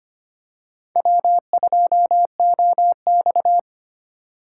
Morse code, heard as a keyed tone, about 25 wpm, 700 Hz. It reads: W2OX
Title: USA Call Signs - 4 Characters - 2x Character Spacing - 25wpm